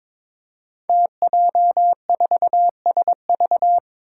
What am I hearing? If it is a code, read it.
Morse code, heard as a keyed tone, about 23 wpm, 700 Hz. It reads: TJ4SV
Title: Five Challenging Letter-Number Combinations 22wpm